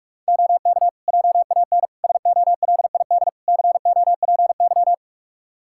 Morse code, heard as a keyed tone, about 46 wpm, 700 Hz. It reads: OK JAN SOLID COPY
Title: K1USN SST QSO Practice 45wpm